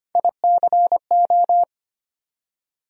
Morse code, heard as a keyed tone, about 25 wpm, 700 Hz. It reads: ICO
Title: USA Call Sign Postfixes - 3 Letters 25wpm